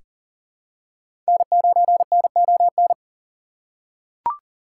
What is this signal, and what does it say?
Morse code, heard as a keyed tone, about 40 wpm, 700 Hz. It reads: N9NON
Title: USA Call Signs - 5 Characters - 6x Character Spacing - 40wpm